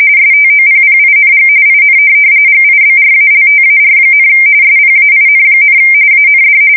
RTTY, about 45.45 baud. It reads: STOPPING IN GIBRALTAR, SPAIN, AND
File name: fsk170 / USS-New-Jersey